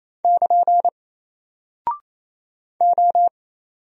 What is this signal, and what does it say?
Morse code, heard as a keyed tone, about 27 wpm, 700 Hz. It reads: <KN> O
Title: Single Character (Letter:Number:Punctuation:Prosign) 28wpm